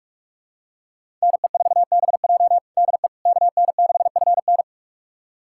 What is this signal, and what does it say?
Morse code, heard as a keyed tone, about 41 wpm, 700 Hz. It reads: NE4BJ DE KN6FN
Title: USA Callsign DE Callsign - 5 Characters - 45wpm